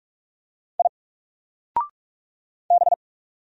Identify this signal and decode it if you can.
Morse code, heard as a keyed tone, about 45 wpm, 700 Hz. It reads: I B
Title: Single Letter-Number Alt 45wpm